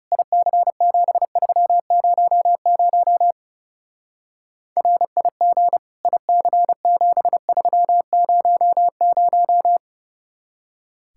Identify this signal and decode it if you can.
Morse code, heard as a keyed tone, about 35 wpm, 700 Hz. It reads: IC7300 RIG IC7300
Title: Speed-Racing Common QSO Phrases - 30wpm